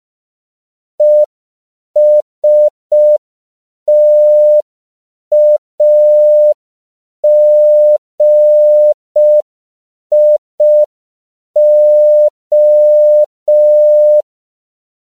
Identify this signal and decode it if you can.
Morse code, heard as a keyed tone, about 5 wpm, 600 Hz. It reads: ESTAGIO
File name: CW / 5PPM